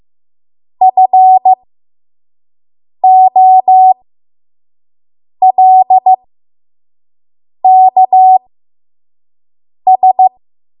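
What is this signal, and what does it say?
Morse code, heard as a keyed tone, about 15 wpm, spaced as for 5 wpm, 750 Hz. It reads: FOLKS
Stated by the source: 5 WPM Morse code practice.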